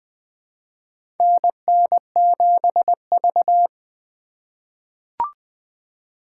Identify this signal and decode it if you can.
Morse code, heard as a keyed tone, about 20 wpm, 700 Hz. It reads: NN7V
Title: USA Call Signs - 4x Character Spacing - 20wpm